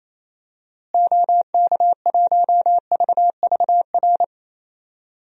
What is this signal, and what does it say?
Morse code, heard as a keyed tone, about 28 wpm, 700 Hz. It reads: OK1VVR
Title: DXCC-100 Call Signs - 6 Characters - Repeated 3x - 28wpm